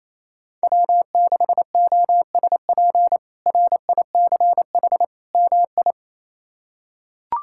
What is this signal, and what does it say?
Morse code, heard as a keyed tone, about 27 wpm, 700 Hz. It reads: W6OSP RICH MI
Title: K1USN SST QSO Practice - 28wpm